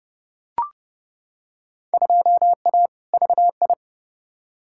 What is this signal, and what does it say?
Morse code, heard as a keyed tone, about 30 wpm, 700 Hz. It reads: 2A VI
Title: Field Day Exchanges Encore - 30wpm